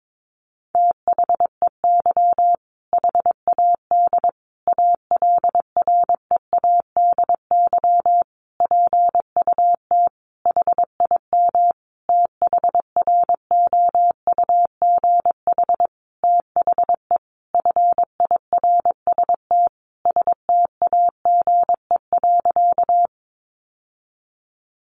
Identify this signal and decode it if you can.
Morse code, heard as a keyed tone, about 22 wpm, 700 Hz. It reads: THEY HAD ALREADY PUT HIM THROUGH THE FIRST STAGE.
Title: Sentences from Top 1200 Words 22wpm